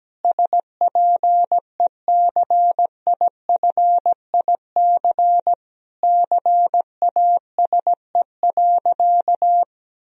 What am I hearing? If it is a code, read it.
Morse code, heard as a keyed tone, about 17 wpm, 700 Hz. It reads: SPECIFIC CASE.
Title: Sentences from Top 1000 Words - Review Part 3 - 17wpm